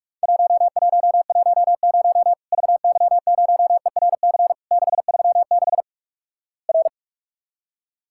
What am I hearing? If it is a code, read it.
Morse code, heard as a keyed tone, about 46 wpm, 700 Hz. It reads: 1110 VY0ERC 636 R
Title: Worked All Europe DX Contest - QTC Exchange - 45wpm